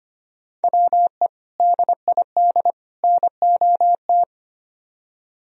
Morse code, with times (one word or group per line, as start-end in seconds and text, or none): 0.6-1.3 WE
1.6-2.7 DID
3.0-4.2 NOT